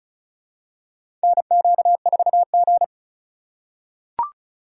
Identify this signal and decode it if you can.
Morse code, heard as a keyed tone, about 36 wpm, 700 Hz. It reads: NQ4G
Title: USA Call Signs - 2x Character Spacing - 35wpm